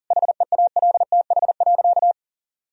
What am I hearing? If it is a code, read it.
Morse code, heard as a keyed tone, about 40 wpm, 700 Hz. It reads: HEALTH.
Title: Sentences from Top 1000 Words - Review Part 3 - 40wpm